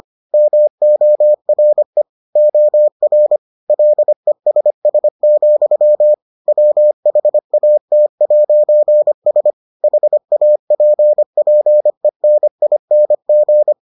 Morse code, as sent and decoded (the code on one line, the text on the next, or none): -- --- .-. / --- .-. / .-.. ... ... --..-- / .-- .... .- - .----. ... / .... .- .--. .--. -. .. -. --.
MORE OR LESS, WHAT'S HAPPENING